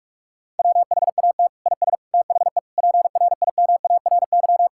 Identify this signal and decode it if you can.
Morse code, heard as a keyed tone, about 41 wpm, 700 Hz. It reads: WHAT IS THE PRIMARY
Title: Sentences from Top 1000 Words - Review Part 5 - 45wpm